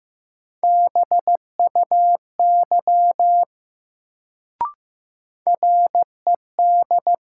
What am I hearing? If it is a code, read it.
Morse code, heard as a keyed tone, about 15 wpm, 700 Hz. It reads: BUY RED
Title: Sentences from Top 700 Words - New Word Drill - 15wpm